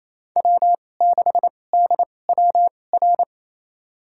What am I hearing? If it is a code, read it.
Morse code, heard as a keyed tone, about 28 wpm, spaced as for 20 wpm, 700 Hz. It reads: W6DWR
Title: USA Call Signs - 5 Characters - 2x Character Spacing - 28wpm